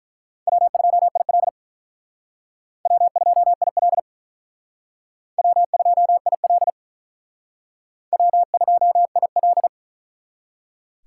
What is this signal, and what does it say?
Morse code, heard as a keyed tone, about 43 wpm, 700 Hz. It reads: W2IL W2IL W2IL W2IL
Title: Speed Racing - USA 1x2 Call Signs 35wpm